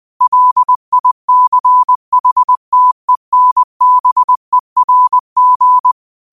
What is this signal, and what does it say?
Morse code, heard as a keyed tone, about 20 wpm, 1000 Hz. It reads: LICHTENBERG
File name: qotdfiles / 20wpm